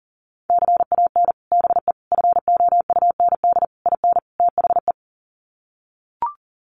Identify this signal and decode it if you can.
Morse code, heard as a keyed tone, about 40 wpm, 700 Hz. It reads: CAN BE FOUND IN THE
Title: Sets of 5 Words Encore 40wpm